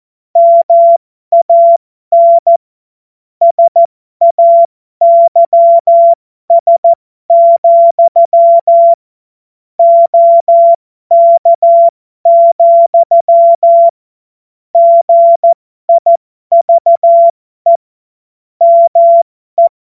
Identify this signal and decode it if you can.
Morse code, heard as a keyed tone, about 14 wpm, spaced as for 12 wpm, 680 Hz. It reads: MAN SAYS, OK, GIVE ME
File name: Morsekurs / Witze